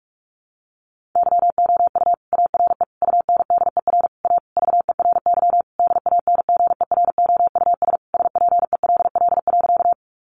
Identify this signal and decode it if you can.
Morse code, heard as a keyed tone, about 44 wpm, 700 Hz. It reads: YOU ARE UNDER A VERY DANGEROUS SPELL.
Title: Sentences from Top 1200 Words 45wpm